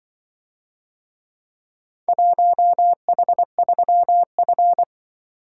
Morse code, heard as a keyed tone, about 24 wpm, 700 Hz. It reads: 1H3F
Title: Speed-Racing - Four Letter-Number Combinations - 20wpm